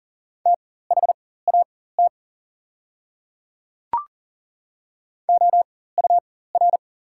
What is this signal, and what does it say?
Morse code, heard as a keyed tone, about 40 wpm, spaced as for 19 wpm, 690 Hz. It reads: THAT OUR
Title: 4x Character Spacing - Top 200 Words - 40wpm